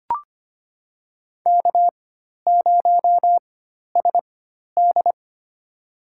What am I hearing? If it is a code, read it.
Morse code, heard as a keyed tone, about 25 wpm, spaced as for 12 wpm, 690 Hz. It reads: K0SD
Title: USA Call Signs - 4 Characters - 4x Character Spacing - 25wpm